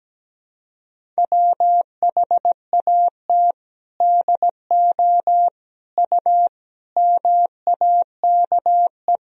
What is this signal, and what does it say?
Morse code, heard as a keyed tone, about 17 wpm, 700 Hz. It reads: WHAT DO U MAKE
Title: Sentences from Top 100 Words and Abbreviations 17wpm